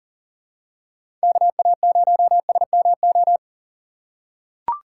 Morse code, heard as a keyed tone, about 37 wpm, 700 Hz. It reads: KA0SMO
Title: USA Call Signs - 6 Characters - 6x Character Spacing - 40wpm